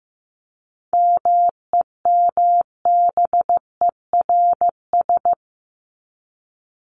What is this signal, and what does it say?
Morse code, heard as a keyed tone, about 15 wpm, 700 Hz. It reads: MEMBERS
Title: ES:S Suffix - Top 100 Words with Base Words 15wpm